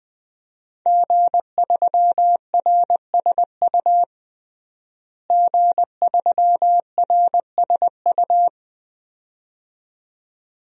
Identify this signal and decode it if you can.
Morse code, heard as a keyed tone, about 21 wpm, 700 Hz. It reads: G3RSU G3RSU
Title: DXCC-100 Call Signs - 5 Characters - Repeated 2x - 20wpm